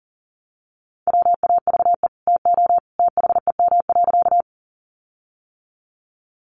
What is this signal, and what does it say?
Morse code, heard as a keyed tone, about 40 wpm, 700 Hz. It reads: WAVE TO THEM.
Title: Sentences from Top 1400 Words 40wpm